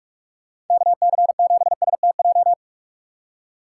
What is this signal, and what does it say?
Morse code, heard as a keyed tone, about 43 wpm, 700 Hz. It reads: KC7STJ
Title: USA Call Signs - 4x Character Spacing - 45wpm